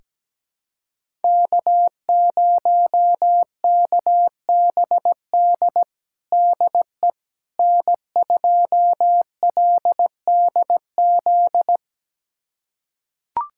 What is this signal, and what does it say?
Morse code, heard as a keyed tone, about 17 wpm, 700 Hz. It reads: K0KBD DE N2LDZ
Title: USA Callsign DE Callsign - 5 Characters - 17wpm